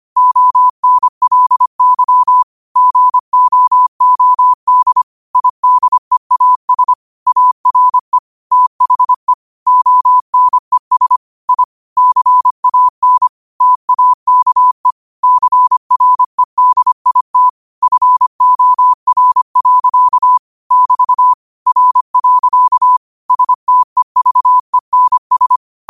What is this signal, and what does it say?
Morse code, heard as a keyed tone, about 25 wpm, 1000 Hz. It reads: ONLY GOOD IDEAS ARE THE ONES I CAN TAKE CREDIT FOR. <BT> R. STEVENS